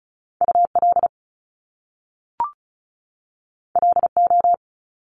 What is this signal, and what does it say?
Morse code, heard as a keyed tone, about 36 wpm, 700 Hz. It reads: UL LO
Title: Top Two Letter Combinations in Words 35wpm